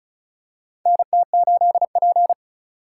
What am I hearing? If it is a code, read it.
Morse code, heard as a keyed tone, about 36 wpm, 700 Hz. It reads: NT8P
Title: USA Call Signs - 4 Characters - 4x Character Spacing - 35wpm